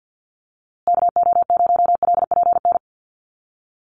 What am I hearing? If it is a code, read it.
Morse code, heard as a keyed tone, about 50 wpm, 700 Hz. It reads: KO0LPN
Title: USA Call Signs Repeated 3x 50wpm